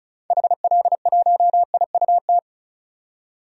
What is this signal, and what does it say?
Morse code, heard as a keyed tone, about 36 wpm, 700 Hz. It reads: HL1IUT
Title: DXCC-100 Call Signs - 6 Characters - 35wpm